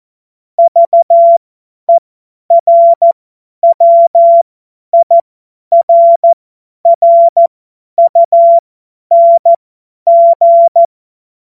Morse code, heard as a keyed tone, about 14 wpm, spaced as for 10 wpm, 680 Hz. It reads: VERWIRRUNG
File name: Morsekurs / Witze / Witz 09. Dezember 2025 50 F70